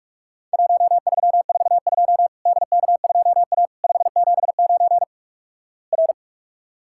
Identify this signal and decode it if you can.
Morse code, heard as a keyed tone, about 47 wpm, 700 Hz. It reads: 1342 DK2A 579 R
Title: Worked All Europe DX Contest - QTC Exchange - 45wpm